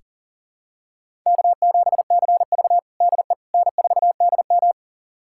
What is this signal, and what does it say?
Morse code, heard as a keyed tone, about 40 wpm, 700 Hz. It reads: K7CV DE N4DM
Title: USA Callsign DE Callsign - 4 Characters - 40wpm